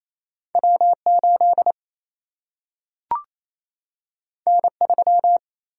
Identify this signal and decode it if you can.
Morse code, heard as a keyed tone, about 28 wpm, 700 Hz. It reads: W 8 N 3
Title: Single-Letter Single-Number 28wpm